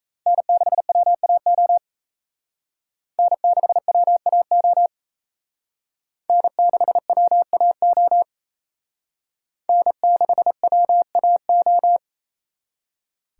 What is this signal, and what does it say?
Morse code, heard as a keyed tone, about 34 wpm, 700 Hz. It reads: N6WAO N6WAO N6WAO N6WAO
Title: Speed Racing - USA Call Signs 28wpm